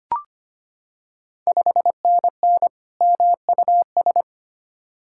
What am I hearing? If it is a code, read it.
Morse code, heard as a keyed tone, about 25 wpm, 700 Hz. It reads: 5NN MUS
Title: Ontario QSO Party Exchange 25wpm